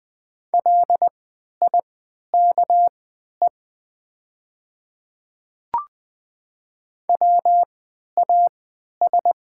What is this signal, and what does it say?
Morse code, heard as a keyed tone, about 20 wpm, spaced as for 11 wpm, 700 Hz. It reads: LIKE WAS
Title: 3x Character Spacing - Top 100 Words - 20wpm